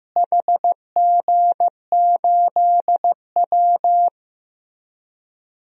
Morse code, heard as a keyed tone, about 15 wpm, 700 Hz. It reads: HG8W
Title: DXCC-100 Call Signs - 4 Characters - Repeated 3x - 15wpm